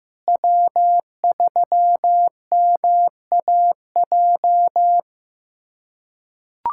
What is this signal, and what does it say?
Morse code, heard as a keyed tone, about 16 wpm, 700 Hz. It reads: W3MAJ
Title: USA Callsign DE Callsign - 5 Characters - 15wpm